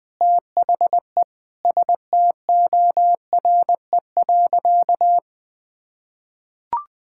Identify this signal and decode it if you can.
Morse code, heard as a keyed tone, about 21 wpm, 700 Hz. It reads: THE STORE.
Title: Sentences with 5-Word Phrases - Spoken Beginning - 20wpm